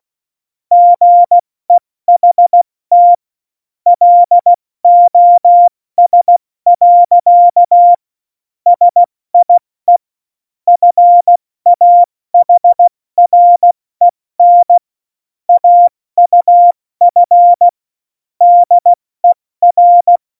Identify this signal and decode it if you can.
Morse code, heard as a keyed tone, about 16 wpm, 700 Hz. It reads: GEHT LOS. SIE FAHREN AUF DER
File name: Morsekurs / Witze / Witz 05. Februar 2019, 70 F80